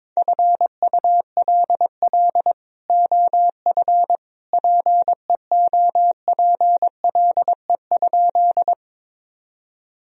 Morse code, heard as a keyed tone, about 22 wpm, 700 Hz. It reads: FULL OF PEOPLE?
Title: Sentences from Top 1000 Words - Review Part 3 - 22wpm